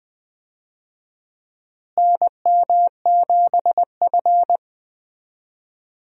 Morse code, heard as a keyed tone, about 20 wpm, 700 Hz. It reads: NM7F
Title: USA Call Signs - 4 Characters - Repeated 3x - 20wpm